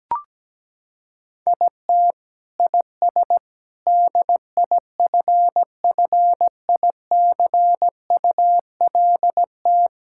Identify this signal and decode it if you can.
Morse code, heard as a keyed tone, about 17 wpm, 700 Hz. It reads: IT IS DIFFICULT
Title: Sentences from Top 1600 Words 17wpm